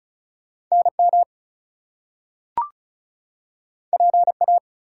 Morse code, heard as a keyed tone, about 36 wpm, 700 Hz. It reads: NM PA
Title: US States and Canadian Provinces 35wpm